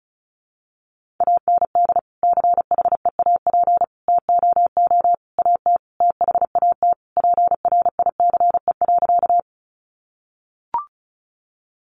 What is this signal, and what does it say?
Morse code, heard as a keyed tone, about 34 wpm, 700 Hz. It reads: AND CHEAP TOO AT THAT PRICE.
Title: Sentences from Top 1200 Words 35wpm